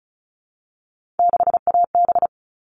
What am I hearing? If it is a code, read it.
Morse code, heard as a keyed tone, about 34 wpm, 700 Hz. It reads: 6AB
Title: All Cut Numbers - 3-Digit 35wpm